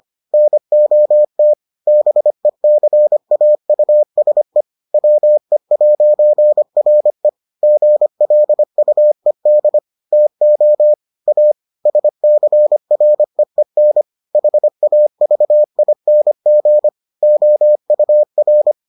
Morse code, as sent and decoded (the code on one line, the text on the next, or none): -. --- - / -... -.-. .- ..- ... / .-- . .----. .-. / --. .-.. ..- . -.. / - --- / .- / ... -.-. .-. . -. / .... .- ...- .. -. --. / --- ..- .-.
NOT BECAUSE WE'RE GLUED TO A SCREEN HAVING OUR